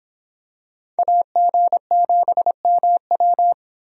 Morse code, heard as a keyed tone, about 24 wpm, 700 Hz. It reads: AG7MW